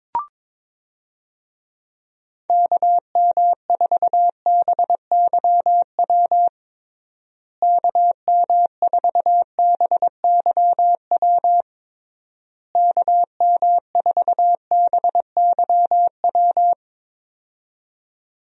Morse code, heard as a keyed tone, about 22 wpm, 700 Hz. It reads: KM4BYW KM4BYW KM4BYW
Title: USA Call Signs - 6 Characters - Repeated 3x - 22wpm